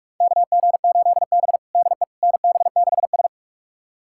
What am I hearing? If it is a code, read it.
Morse code, heard as a keyed tone, about 46 wpm, 700 Hz. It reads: KG8B DE NB6S
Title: USA Callsign DE Callsign - 4 Characters - 45wpm